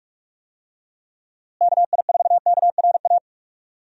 Morse code, heard as a keyed tone, about 48 wpm, 700 Hz. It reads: KI4KRA
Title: USA Call Signs - 6 Characters - Repeated 3x - 45wpm